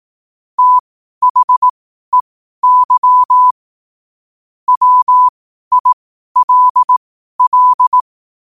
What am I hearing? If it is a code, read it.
Morse code, heard as a keyed tone, about 18 wpm, spaced as for 12 wpm, 1000 Hz. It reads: THEY WILL
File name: qotdfiles / 12wpm